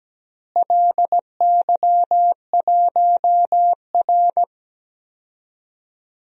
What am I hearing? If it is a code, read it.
Morse code, heard as a keyed tone, about 17 wpm, 700 Hz. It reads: LY1R
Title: DXCC-100 Call Signs - 4 Characters - Repeated 2x - 17wpm